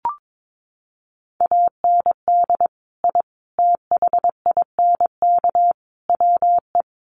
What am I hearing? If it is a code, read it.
Morse code, heard as a keyed tone, about 22 wpm, 700 Hz. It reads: AND I THINK WE
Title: Sets of 4 Words 22wpm